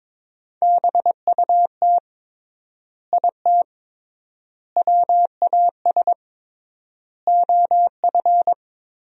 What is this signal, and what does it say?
Morse code, heard as a keyed tone, about 21 wpm, 700 Hz. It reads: BUT IT WAS OF
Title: 3x Word-Spacing - Sentences from Top 100 Words - 22wpm